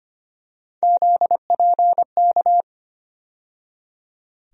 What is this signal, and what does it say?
Morse code, heard as a keyed tone, about 25 wpm, 700 Hz. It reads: ZPK
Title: USA Call Sign Postfixes - 3 Letters 25wpm